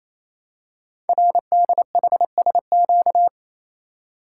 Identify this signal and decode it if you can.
Morse code, heard as a keyed tone, about 27 wpm, 700 Hz. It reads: RDHSQ
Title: Code Groups - 5 Letters - 28wpm